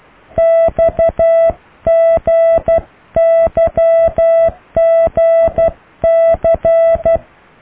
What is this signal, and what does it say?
Morse code, heard as a keyed tone, about 12 wpm, 660 Hz. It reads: XGYGC